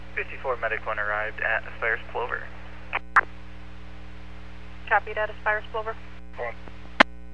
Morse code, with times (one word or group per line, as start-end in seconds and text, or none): none